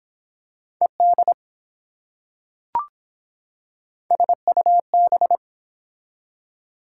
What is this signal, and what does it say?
Morse code, heard as a keyed tone, about 24 wpm, 700 Hz. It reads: ED SUB